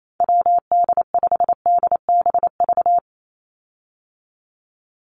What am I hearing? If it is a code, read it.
Morse code, heard as a keyed tone, about 28 wpm, 700 Hz. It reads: WD5DBV
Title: Call Signs - 28wpm